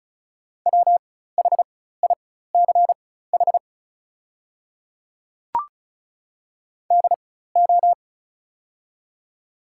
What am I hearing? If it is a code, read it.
Morse code, heard as a keyed tone, about 36 wpm, spaced as for 16 wpm, 700 Hz. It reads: WHICH DO
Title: 4x Character Spacing - Top 100 Words - 35wpm